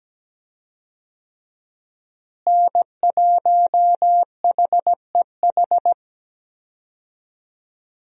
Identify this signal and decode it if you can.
Morse code, heard as a keyed tone, about 17 wpm, 700 Hz. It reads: N1HEH
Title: USA Call Signs - 5 Characters - Repeated 2x - 17wpm